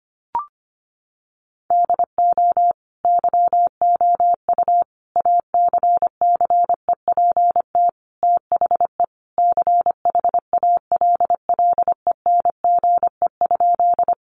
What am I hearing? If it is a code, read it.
Morse code, heard as a keyed tone, about 25 wpm, 700 Hz. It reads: DO YOU ACCEPT THE CHALLENGE?
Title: Sentences from Top 1300 Words 25wpm